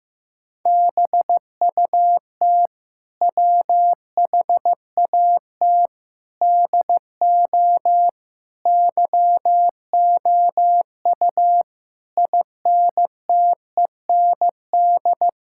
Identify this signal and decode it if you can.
Morse code, heard as a keyed tone, about 15 wpm, 700 Hz. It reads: BUT WHAT DO YOU INTEND
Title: Sentences from Top 1600 Words 15wpm